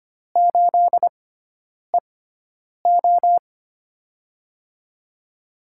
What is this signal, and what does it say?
Morse code, heard as a keyed tone, about 25 wpm, spaced as for 9 wpm, 700 Hz. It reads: 8EO
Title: USA Call Signs - 6x Character Spacing - 25wpm